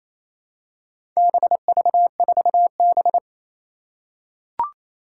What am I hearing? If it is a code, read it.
Morse code, heard as a keyed tone, about 27 wpm, 700 Hz. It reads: BV4B
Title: Four Challenging Letter-Number Combinations 28wpm